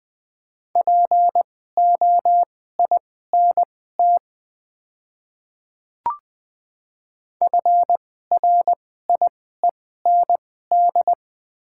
Morse code, heard as a keyed tone, about 20 wpm, spaced as for 14 wpm, 700 Hz. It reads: POINT FRIEND
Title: 2x Character Spacing - Top 300 Words - 20wpm